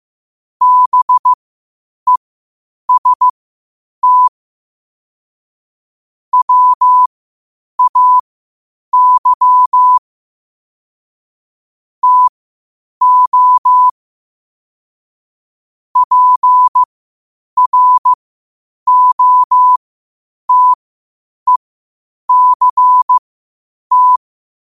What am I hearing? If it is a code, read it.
Morse code, heard as a keyed tone, about 15 wpm, spaced as for 8 wpm, 1000 Hz. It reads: BEST WAY TO PROTECT
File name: qotdfiles / 8wpm